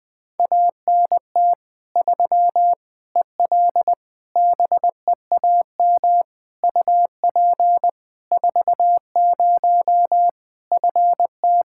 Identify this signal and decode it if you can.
Morse code, heard as a keyed tone, about 20 wpm, 700 Hz. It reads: ANT 3 EL BEAM UP 40 FT
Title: Practice QSO - Exchange 2 - Info - 20wpm